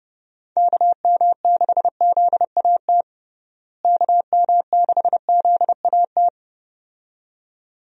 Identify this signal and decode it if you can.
Morse code, heard as a keyed tone, about 30 wpm, 700 Hz. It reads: KM6ZAT KM6ZAT
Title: USA Call Signs - 6 Characters - Repeated 2x - 30wpm